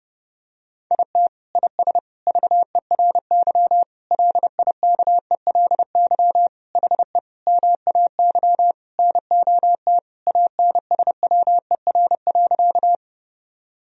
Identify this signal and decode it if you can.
Morse code, heard as a keyed tone, about 32 wpm, 700 Hz. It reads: IT IS VERY LIKELY HE MAY NOT ANSWER.
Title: Sentences from Top 1000 Words - Review Part 4 - 30wpm